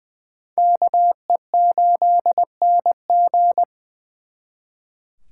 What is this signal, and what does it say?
Morse code, heard as a keyed tone, about 24 wpm, 700 Hz. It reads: KE8NG
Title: Speed Racing - USA 2x2 Call Signs 20wpm